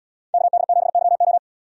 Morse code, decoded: KD7YO

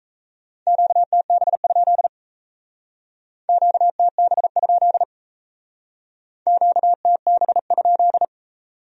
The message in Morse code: QTB? QTB? QTB?

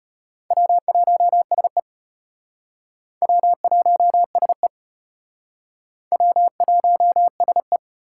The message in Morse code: W1SE W1SE W1SE

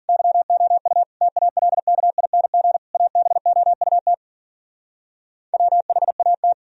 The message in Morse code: YOU TALKING ABOUT WHAT

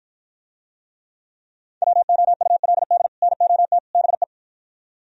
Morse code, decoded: WOULD NOT BE

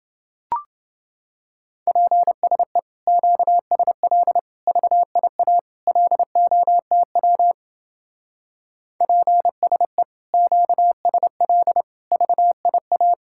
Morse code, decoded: PSE QSL VIA LOTW PSE QSL VIA

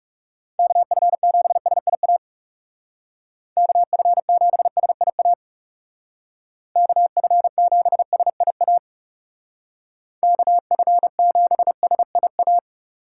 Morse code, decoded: KF7SIA KF7SIA KF7SIA KF7SIA

